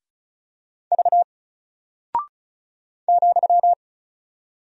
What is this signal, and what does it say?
Morse code, heard as a keyed tone, about 35 wpm, 700 Hz. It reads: V ,
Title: Single Character (Letter:Number:Punctuation:Prosign) 35wpm